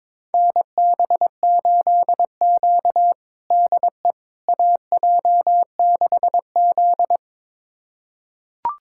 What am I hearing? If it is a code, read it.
Morse code, heard as a keyed tone, about 22 wpm, 700 Hz. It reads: NB8Q DE AJ6Z
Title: USA Callsign DE Callsign - 4 Characters - 22wpm